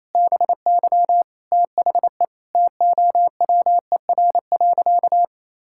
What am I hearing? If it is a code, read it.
Morse code, heard as a keyed tone, about 28 wpm, 700 Hz. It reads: BY THE TOWER.
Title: Pangram Challenge v2 - 28wpm